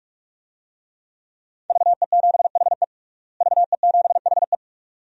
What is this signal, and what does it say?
Morse code, heard as a keyed tone, about 46 wpm, 700 Hz. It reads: VE7HE VE7HE
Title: DXCC-100 Call Signs - 5 Characters - Repeated 2x - 45wpm